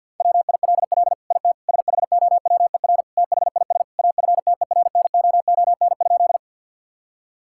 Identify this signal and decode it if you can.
Morse code, heard as a keyed tone, about 50 wpm, 700 Hz. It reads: WILL IT SHOWER THIS AFTERNOON?
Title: Sentences from Top 1600 Words 50wpm